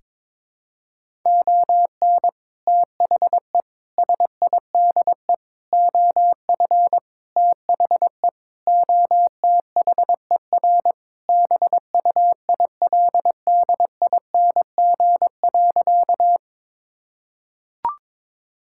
Morse code, heard as a keyed tone, about 22 wpm, 700 Hz. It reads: ON THE SIDE OF THE OTHER BUILDING.
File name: practice / 22wpm / Sentences with 3-Word Phrases - 22wpm